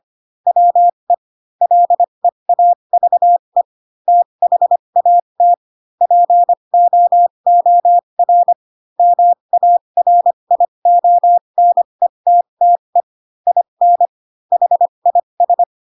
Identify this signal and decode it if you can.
Morse code, heard as a keyed tone, about 25 wpm, spaced as for 22 wpm, 700 Hz. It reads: WE LEAVE THAT POOR MARIONETTE IN HIS
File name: Pinocchio / Pinocchio ch3 18 25 ik1hge